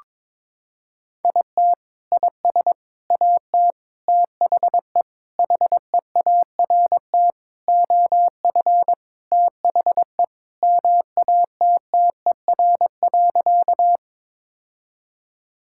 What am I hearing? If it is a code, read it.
Morse code, heard as a keyed tone, about 22 wpm, 700 Hz. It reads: IT IS AT THE HEART OF THE MATTER.
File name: practice / 22wpm / Sentences with 4-Word Phrases - 22wpm